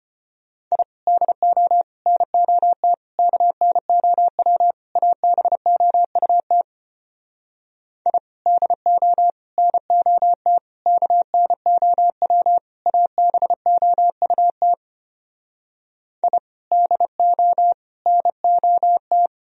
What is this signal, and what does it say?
Morse code, heard as a keyed tone, about 33 wpm, 700 Hz. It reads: I DO NOT KNOW ABOUT I DO NOT KNOW ABOUT I DO NOT